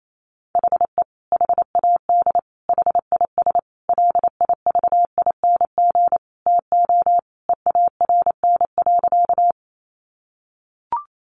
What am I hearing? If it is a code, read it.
Morse code, heard as a keyed tone, about 28 wpm, 700 Hz. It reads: HE HAD HIS LIVING TO EARN.
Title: Sentences from Top 1500 Words 28wpm